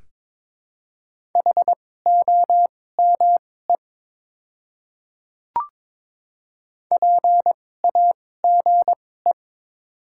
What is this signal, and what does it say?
Morse code, heard as a keyed tone, about 22 wpm, spaced as for 16 wpm, 700 Hz. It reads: HOME PAGE
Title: 2x Character Spacing - Top 300 Words - 22wpm